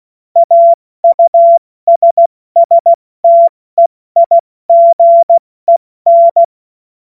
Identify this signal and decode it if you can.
Morse code, heard as a keyed tone, about 16 wpm, 680 Hz. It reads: AUSSTEIGEN
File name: Morsekurs / Witze / Witz 31. August 2021, 70 F80